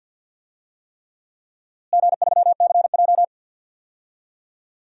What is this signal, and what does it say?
Morse code, heard as a keyed tone, about 52 wpm, 700 Hz. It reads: G3CJ
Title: DXCC-100 Call Signs - 4 Characters - Repeated 3x - 50wpm